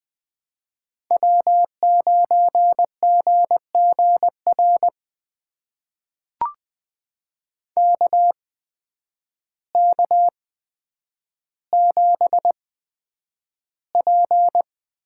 Morse code, heard as a keyed tone, about 20 wpm, spaced as for 9 wpm, 700 Hz. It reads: W9GGR KK7P
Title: USA Call Signs - 5 Characters - 8x Character Spacing - 20wpm